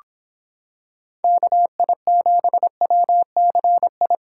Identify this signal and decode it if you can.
Morse code, heard as a keyed tone, about 26 wpm, 700 Hz. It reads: KI7WCI